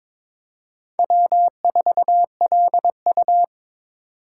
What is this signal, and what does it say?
Morse code, heard as a keyed tone, about 26 wpm, 700 Hz. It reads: W4LU
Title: Speed Racing - USA 1x2 Call Signs 22wpm